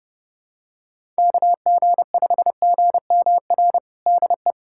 Morse code, heard as a keyed tone, about 32 wpm, 700 Hz. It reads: KG5GMR DE
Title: USA Callsign DE Callsign 30wpm